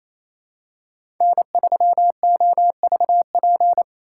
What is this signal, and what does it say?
Morse code, heard as a keyed tone, about 28 wpm, 700 Hz. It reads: N3OVP
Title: USA Call Signs - 5 Characters - 2x Character Spacing - 28wpm